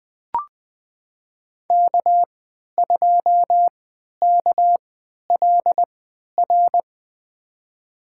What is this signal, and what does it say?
Morse code, heard as a keyed tone, about 20 wpm, spaced as for 11 wpm, 700 Hz. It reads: K2KLR
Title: USA Call Signs - 5 Characters - 3x Character Spacing - 20wpm